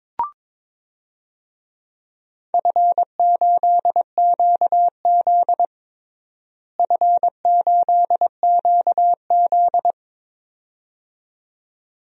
Morse code, heard as a keyed tone, about 22 wpm, 700 Hz. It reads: F8QZ F8QZ
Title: DXCC-100 Call Signs - 4 Characters - Repeated 2x - 22wpm